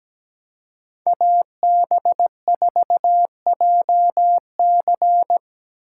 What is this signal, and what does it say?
Morse code, heard as a keyed tone, about 17 wpm, 700 Hz. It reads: AB4JC
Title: USA Call Signs - 3x Character Spacing - 17wpm